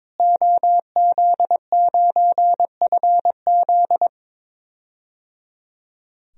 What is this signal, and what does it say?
Morse code, heard as a keyed tone, about 22 wpm, 700 Hz. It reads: OZ9FZ
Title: DXCC-100 Call Signs - 5 Characters - Repeated 3x - 22wpm